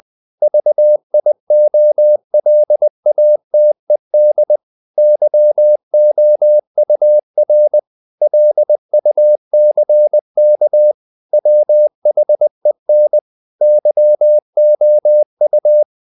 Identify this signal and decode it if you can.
Morse code, heard as a keyed tone, about 20 wpm, 600 Hz. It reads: VIOLATED YOUR LUCK WHEN YOU
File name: the old man and the sea part 2 20wpm